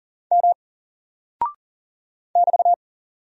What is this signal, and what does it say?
Morse code, heard as a keyed tone, about 40 wpm, 700 Hz. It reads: M <BT>
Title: Single Character (Letter:Number:Punctuation:Prosign) Alt 40wpm